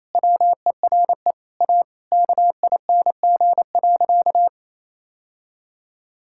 Morse code, heard as a keyed tone, about 27 wpm, 700 Hz. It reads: WERE A KING.
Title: Sentences from Top 1000 Words - Review Part 6 - 28wpm